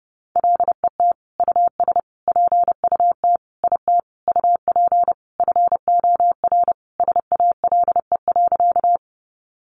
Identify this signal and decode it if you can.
Morse code, heard as a keyed tone, about 30 wpm, 700 Hz. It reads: LET US PUT IT UP FOR SALE.
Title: Sentences from Top 1400 Words 30wpm